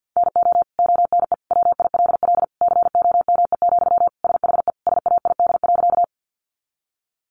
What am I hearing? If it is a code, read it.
Morse code, heard as a keyed tone, about 50 wpm, 700 Hz. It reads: NO ONE WILL COME, SHE SAID.
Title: Sentences from Top 100 Words 50wpm